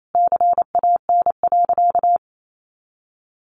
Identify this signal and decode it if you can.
Morse code, heard as a keyed tone, about 28 wpm, 700 Hz. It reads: CAN.
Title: Sentences from Top 1400 Words 28wpm